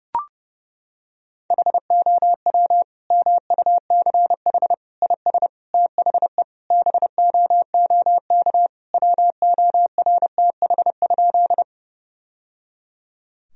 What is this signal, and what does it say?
Morse code, heard as a keyed tone, about 30 wpm, 700 Hz. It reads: HOW MUCH IS THE BOOK WORTH?
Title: Sentences from Top 1000 Words - Review Part 1 - 30wpm